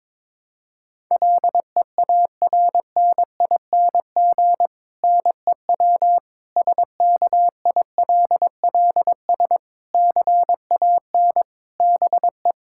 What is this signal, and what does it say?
Morse code, heard as a keyed tone, about 22 wpm, 700 Hz. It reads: LEARNING NEW SKILLS CAN BE
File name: Difficult Sentences with Hints - 22wpm